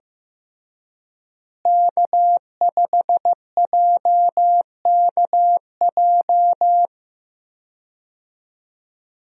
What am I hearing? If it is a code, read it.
Morse code, heard as a keyed tone, about 16 wpm, 700 Hz. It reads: K5JKJ
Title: USA Call Signs - 5 Characters - Repeated 3x - 15wpm